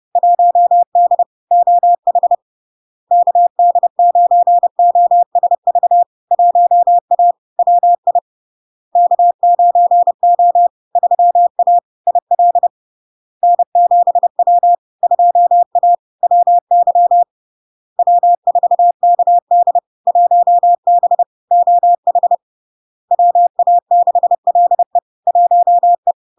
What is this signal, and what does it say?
Morse code, heard as a keyed tone, about 30 wpm, 700 Hz. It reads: 1D OH KD9OSV 1A WI K9O 3A IL N7W 2A WY W4KD 1B OH WA6LE 1E
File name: LARK-FD-2022-Log-in-CW-30wpm-700-Hz